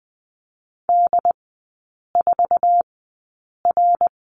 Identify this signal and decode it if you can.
Morse code, heard as a keyed tone, about 20 wpm, spaced as for 8 wpm, 700 Hz. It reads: D4R
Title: 107 Lesson 18e - Letter m - Three Characters 20wpm